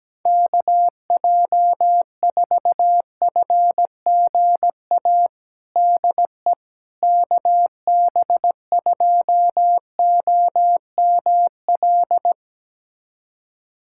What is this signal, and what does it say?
Morse code, heard as a keyed tone, about 17 wpm, 700 Hz. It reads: KJ4FGA DE KB2OML
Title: USA Callsign DE Callsign - 6 Characters - 17wpm